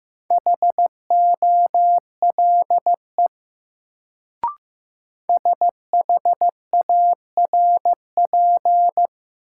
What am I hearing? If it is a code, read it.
Morse code, heard as a keyed tone, about 15 wpm, 700 Hz. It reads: HOLE SHARP
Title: Sentences from Top 1600 Words - New Word Drill Alt - 15wpm